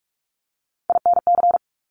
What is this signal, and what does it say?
Morse code, heard as a keyed tone, about 46 wpm, 700 Hz. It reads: INC
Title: Top Three Letter Combinations in Words 45wpm